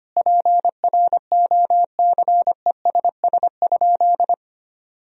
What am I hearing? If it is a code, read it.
Morse code, heard as a keyed tone, about 25 wpm, 700 Hz. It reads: PROCESS?
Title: Sentences from Top 1000 Words - Review Part 3 - 25wpm